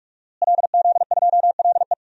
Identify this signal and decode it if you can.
Morse code, heard as a keyed tone, about 46 wpm, 700 Hz. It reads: LZ2LE
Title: DXCC-100 Call Signs - 5 Characters - Repeated 2x - 45wpm